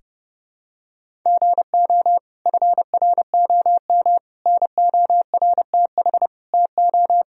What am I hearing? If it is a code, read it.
Morse code, heard as a keyed tone, about 30 wpm, 700 Hz. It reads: GO FROM NORTH TO
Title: Sentences from Top 1000 Words - Review Part 2 - 30wpm